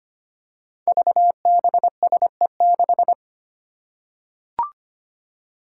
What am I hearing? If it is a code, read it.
Morse code, heard as a keyed tone, about 25 wpm, 700 Hz. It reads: VBSE6
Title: Five Challenging Letter-Number Combinations 25wpm